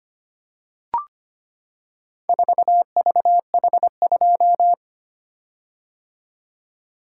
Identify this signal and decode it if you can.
Morse code, heard as a keyed tone, about 25 wpm, 700 Hz. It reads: 4VH2
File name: practice / 25wpm / Four Letter-Number Combinations - 25wpm